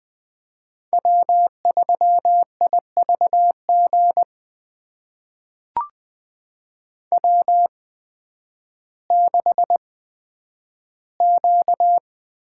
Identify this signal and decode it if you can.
Morse code, heard as a keyed tone, about 20 wpm, spaced as for 11 wpm, 700 Hz. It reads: W3IVG W6Q
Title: USA Call Signs - 8x Character Spacing - 20wpm